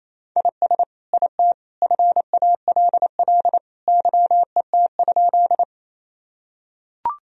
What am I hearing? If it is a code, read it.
Morse code, heard as a keyed tone, about 28 wpm, 700 Hz. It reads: IS IT FALL YET?
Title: Sentences from Top 1000 Words - Review Part 2 - 28wpm